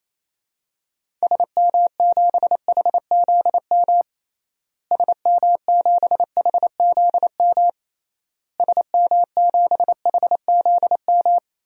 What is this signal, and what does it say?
Morse code, heard as a keyed tone, about 28 wpm, 700 Hz. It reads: SM7HZM SM7HZM SM7HZM
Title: DXCC-100 Call Signs - 6 Characters - Repeated 3x - 28wpm